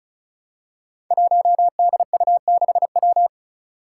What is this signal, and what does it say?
Morse code, heard as a keyed tone, about 34 wpm, 700 Hz. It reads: 1DU6W
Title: Five Challenging Letter-Number Combinations 35wpm